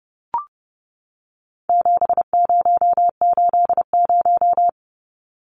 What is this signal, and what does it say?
Morse code, heard as a keyed tone, about 30 wpm, 700 Hz. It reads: 7080
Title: Numbers - 4-Digit 30wpm